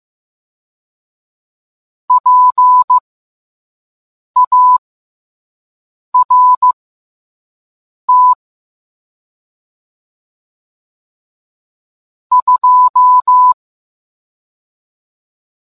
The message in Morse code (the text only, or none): PART 2